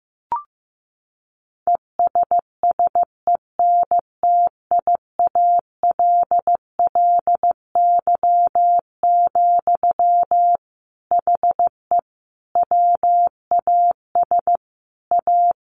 Morse code, heard as a keyed tone, about 15 wpm, 700 Hz. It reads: ESSENTIALLY, HE WAS A
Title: Sentences from Top 1300 Words 15wpm